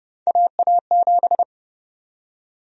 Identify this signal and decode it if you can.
Morse code, heard as a keyed tone, about 31 wpm, 700 Hz. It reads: AA7
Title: USA Call Sign Prefixes - Repeated 2x - 30wpm